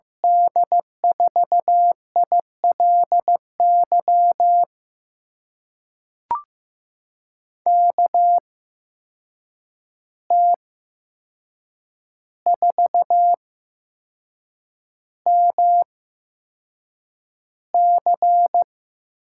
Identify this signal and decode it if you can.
Morse code, heard as a keyed tone, about 15 wpm, spaced as for 6 wpm, 700 Hz. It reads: D4ILY KT4MC
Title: USA Call Signs - 8x Character Spacing - 15wpm